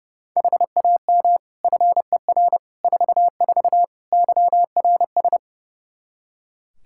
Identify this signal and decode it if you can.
Morse code, heard as a keyed tone, about 32 wpm, 700 Hz. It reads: HAM FER 44 YRS
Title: K1USN SST QSO Practice 30wpm